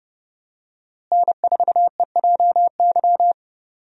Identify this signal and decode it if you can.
Morse code, heard as a keyed tone, about 30 wpm, 700 Hz. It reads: N4EJY
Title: USA Call Signs - 5 Characters - Repeated 3x - 30wpm